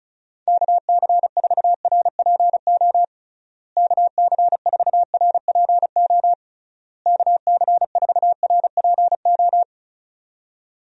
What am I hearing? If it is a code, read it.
Morse code, heard as a keyed tone, about 34 wpm, 700 Hz. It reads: KC4RPO KC4RPO KC4RPO
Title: USA Call Signs - 6 Characters - Repeated 3x - 35wpm